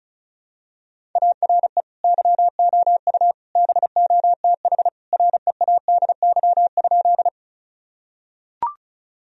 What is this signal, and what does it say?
Morse code, heard as a keyed tone, about 36 wpm, 700 Hz. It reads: ARE YOU BOTH READY?
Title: Sentences from Top 1000 Words - Review Part 1 - 35wpm